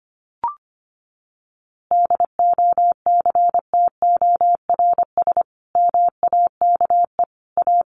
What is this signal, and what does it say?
Morse code, heard as a keyed tone, about 25 wpm, 700 Hz. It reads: DOCTORS MAKE A
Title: Easy Phrases 25wpm